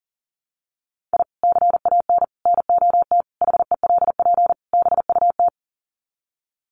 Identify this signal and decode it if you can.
Morse code, heard as a keyed tone, about 40 wpm, 700 Hz. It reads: I CAN NOT HELP BUT
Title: Sets of 5 Words Encore2 40wpm